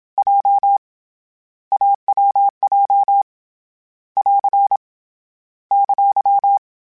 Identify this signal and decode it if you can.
Morse code, heard as a keyed tone, about 27 wpm, 800 Hz. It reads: J AWJ <AR> !
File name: Difficult Combos / Difficult Combos-mp3s / AWJ @25